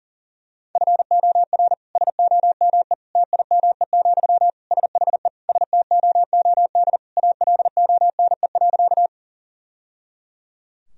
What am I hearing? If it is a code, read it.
Morse code, heard as a keyed tone, about 37 wpm, 700 Hz. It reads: FOR SOME TIME, SHE STOOD ALONE.
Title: Sentences from Top 1000 Words - Review Part 2 - 40wpm